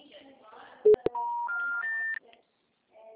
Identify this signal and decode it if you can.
Special information tone, with speech.
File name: out- -2010- - - .67302